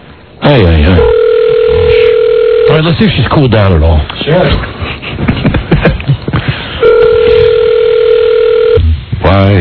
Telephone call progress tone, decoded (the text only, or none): ringback tone